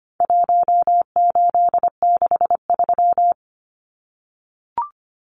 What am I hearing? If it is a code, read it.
Morse code, heard as a keyed tone, about 25 wpm, 700 Hz. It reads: 1863
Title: Numbers - Contest Style - 4-Digit 25wpm